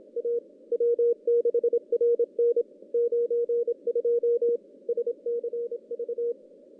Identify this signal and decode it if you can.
Morse code, heard as a keyed tone, about 26 wpm, 470 Hz. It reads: A W6RN 92 SCV
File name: CW / CW-SS-k5tr- -1700